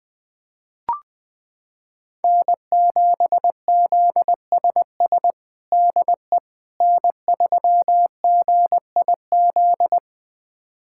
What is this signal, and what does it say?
Morse code, heard as a keyed tone, about 20 wpm, 700 Hz. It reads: N7ZSS DE N3GIZ
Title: USA Callsign DE Callsign - 5 Characters - 20wpm